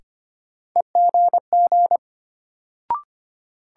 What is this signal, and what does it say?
Morse code, heard as a keyed tone, about 25 wpm, 700 Hz. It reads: EGG
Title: Sentences from Top 1000 Words - Review Part 7 - New Word Drill - 25wpm